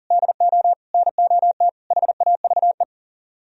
Morse code, heard as a keyed tone, about 52 wpm, 700 Hz. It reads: DO NOT HAVE